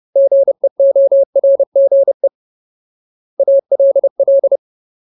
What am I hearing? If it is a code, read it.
Morse code, heard as a keyed tone, about 30 wpm, 550 Hz. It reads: GEORGE ALL
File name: Ultimate Word List / 30WPM Ultimate